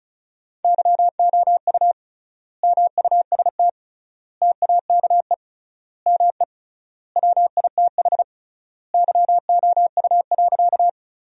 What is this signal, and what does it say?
Morse code, heard as a keyed tone, about 36 wpm, 700 Hz. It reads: YOU MUST TAKE ME WITH YOU.
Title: 3x Word-Spacing - Sentences from Top 200 Words - 35wpm